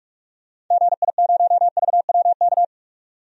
Morse code, heard as a keyed tone, about 42 wpm, 700 Hz. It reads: GI0VWK
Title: DXCC-100 Call Signs - 6 Characters - 45wpm